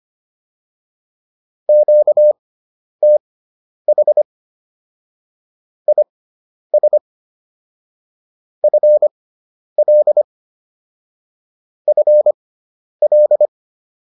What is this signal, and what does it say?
Morse code, heard as a keyed tone, about 25 wpm, spaced as for 10 wpm, 600 Hz. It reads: QTH IS FL FL